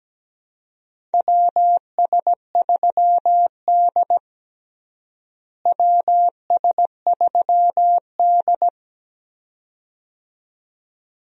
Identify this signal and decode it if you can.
Morse code, heard as a keyed tone, about 17 wpm, 700 Hz. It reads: WS3D WS3D
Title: USA Call Signs - 4 Characters - Repeated 2x - 17wpm